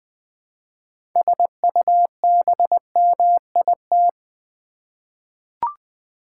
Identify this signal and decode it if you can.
Morse code, heard as a keyed tone, about 20 wpm, 700 Hz. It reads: SUBMIT
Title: Six Letter Words - 20wpm